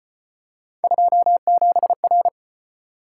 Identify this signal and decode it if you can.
Morse code, heard as a keyed tone, about 32 wpm, 700 Hz. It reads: 27R